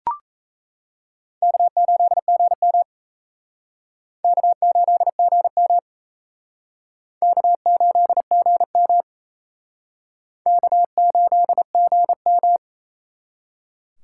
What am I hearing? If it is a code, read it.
Morse code, heard as a keyed tone, about 35 wpm, 700 Hz. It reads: K8GM K8GM K8GM K8GM
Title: Speed Racing - USA 1x2 Call Signs 28wpm